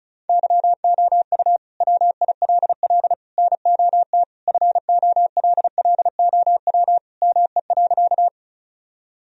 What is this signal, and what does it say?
Morse code, heard as a keyed tone, about 36 wpm, 700 Hz. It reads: YOU WILL NOT FOLLOW ME.
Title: Sentences from Top 1000 Words - Review Part 1 - 35wpm